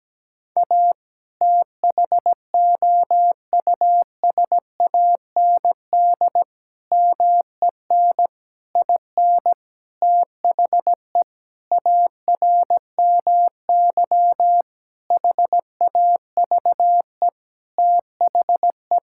Morse code, heard as a keyed tone, about 17 wpm, 700 Hz. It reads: A THOUSAND MEN IN THE ARMY HAVE THE
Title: Sentences from Top 1000 Words - Review Part 6 - 17wpm